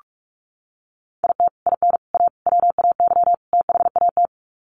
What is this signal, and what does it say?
Morse code, heard as a keyed tone, about 44 wpm, 700 Hz. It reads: IT IN A WAY THAT
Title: Sets of 5 Words Encore2 45wpm